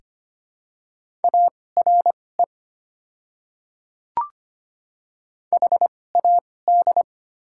Morse code, heard as a keyed tone, about 25 wpm, spaced as for 18 wpm, 690 Hz. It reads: ARE HAD
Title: 2x Character Spacing - Top 100 Words - 25wpm